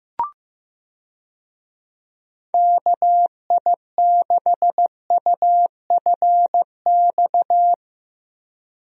Morse code, heard as a keyed tone, about 15 wpm, 700 Hz. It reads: KI6UFX
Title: USA Call Signs - 6 Characters - Repeated 2x - 15wpm